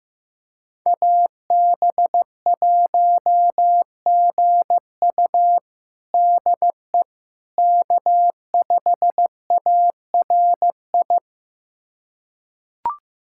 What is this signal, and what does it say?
Morse code, heard as a keyed tone, about 14 wpm, 700 Hz. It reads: AB1GU DE K5ARI
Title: USA Callsign DE Callsign - 5 Characters - 15wpm